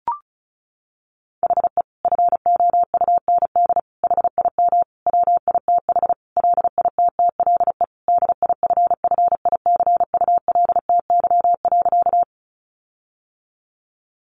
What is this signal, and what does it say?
Morse code, heard as a keyed tone, about 34 wpm, 700 Hz. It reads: HE FOUND HIM WITH LITTLE DIFFICULTY.
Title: Sentences from Top 1500 Words 35wpm